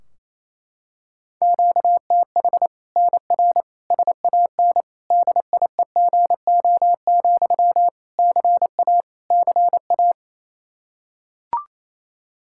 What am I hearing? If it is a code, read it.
Morse code, heard as a keyed tone, about 28 wpm, 700 Hz. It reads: QTH NR SAN DIEGO, CA CA
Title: Practice QSO - Exchange 1 - Beginning - 28wpm